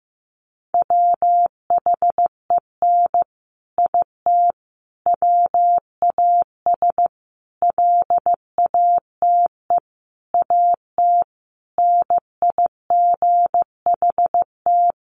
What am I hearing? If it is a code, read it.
Morse code, heard as a keyed tone, about 15 wpm, 700 Hz. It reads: WHEN IT WAS LATE AT NIGHT
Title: Sentences from Top 600 Words 15wpm